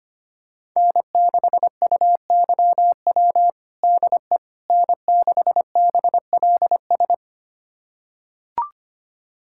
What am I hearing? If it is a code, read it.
Morse code, heard as a keyed tone, about 25 wpm, 700 Hz. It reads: N6UYW DE N6BLS
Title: USA Callsign DE Callsign - 5 Characters - 25wpm